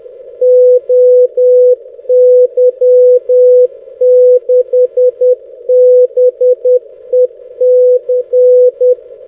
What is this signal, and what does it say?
Morse code, heard as a keyed tone, about 10 wpm, 500 Hz. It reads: OY6BEC
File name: 6m-oy6bec-tf1six-jw7six-1810z